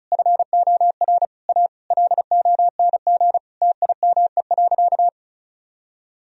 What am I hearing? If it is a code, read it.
Morse code, heard as a keyed tone, about 36 wpm, 700 Hz. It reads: FOR A LONG TIME.
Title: Sentences from Top 1000 Words - Review Part 1 - 35wpm